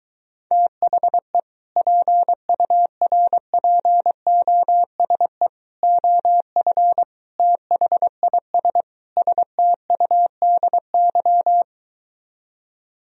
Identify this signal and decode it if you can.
Morse code, heard as a keyed tone, about 23 wpm, 700 Hz. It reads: THE PURPOSE OF THIS STUDY